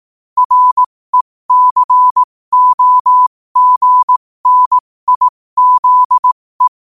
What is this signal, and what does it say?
Morse code, heard as a keyed tone, about 18 wpm, spaced as for 15 wpm, 1000 Hz. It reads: RECOGNIZE
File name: qotdfiles / 15wpm